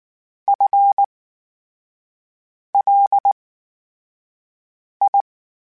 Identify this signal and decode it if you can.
Morse code, heard as a keyed tone, about 19 wpm, spaced as for 5 wpm, 800 Hz. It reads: FLI
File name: Special INGs / Special INGs-mp3s / VerySlowToFaster Ing @18